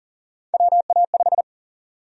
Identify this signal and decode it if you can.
Morse code, heard as a keyed tone, about 40 wpm, 700 Hz. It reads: WA5
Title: USA Call Sign Prefixes - Repeated 2x - 40wpm